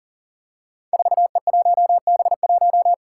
Morse code, heard as a keyed tone, about 42 wpm, 690 Hz. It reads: 4E1B1
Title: Five Challenging Letter-Number Combinations 40wpm